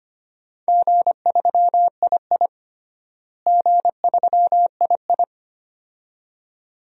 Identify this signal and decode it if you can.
Morse code, heard as a keyed tone, about 25 wpm, 700 Hz. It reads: G3II G3II
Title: DXCC-100 Call Signs - 4 Characters - Repeated 2x - 25wpm